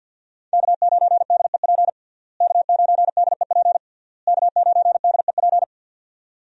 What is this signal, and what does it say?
Morse code, heard as a keyed tone, about 47 wpm, 700 Hz. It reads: K9DEP K9DEP K9DEP
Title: USA Call Signs - 5 Characters - Repeated 3x - 50wpm